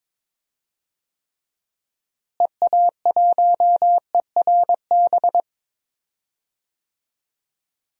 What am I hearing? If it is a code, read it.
Morse code, heard as a keyed tone, about 21 wpm, 700 Hz. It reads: EA1ERB
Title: DXCC-100 Call Signs - 6 Characters - Repeated 3x - 22wpm